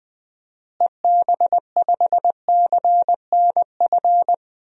Morse code, heard as a keyed tone, about 20 wpm, 700 Hz. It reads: EB5CNF
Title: DXCC-100 Call Signs - 6 Characters - Repeated 2x - 20wpm